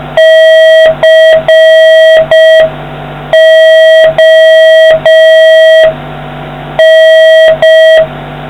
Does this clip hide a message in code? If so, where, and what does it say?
Yes: Morse code from 0.2 to 8.0 s, heard as a keyed tone, about 5 wpm, 1950 Hz, with noise. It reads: CON